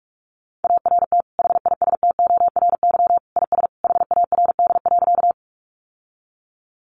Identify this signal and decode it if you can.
Morse code, heard as a keyed tone, about 44 wpm, 700 Hz. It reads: ART HISTORY IS HARD.
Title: Sentences from Top 600 Words 45wpm